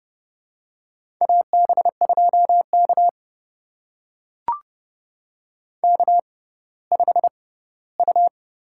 Morse code, heard as a keyed tone, about 32 wpm, spaced as for 17 wpm, 700 Hz. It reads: AB2K K5U
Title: USA Call Signs - 4 Characters - 6x Character Spacing - 30wpm